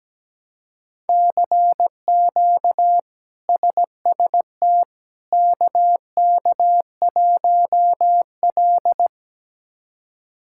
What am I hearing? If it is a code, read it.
Morse code, heard as a keyed tone, about 17 wpm, 700 Hz. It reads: CQ SST KK1L
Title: K1USN SST QSO Practice - 17wpm